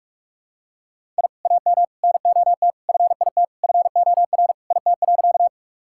Morse code, heard as a keyed tone, about 46 wpm, 700 Hz. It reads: I AM NOT FIT FOR IT.
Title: Sentences from Top 1000 Words - Review Part 5 - 45wpm